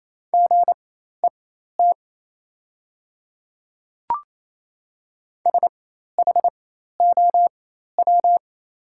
Morse code, heard as a keyed tone, about 28 wpm, spaced as for 13 wpm, 700 Hz. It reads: GET SHOW
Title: 4x Character Spacing - Top 200 Words - 28wpm